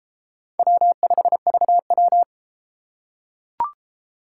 Morse code, heard as a keyed tone, about 33 wpm, 700 Hz. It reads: W5VW